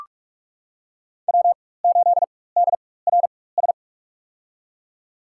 Morse code, heard as a keyed tone, about 46 wpm, spaced as for 21 wpm, 700 Hz. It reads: W8DRS
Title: USA Call Signs - 5 Characters - 4x Character Spacing - 45wpm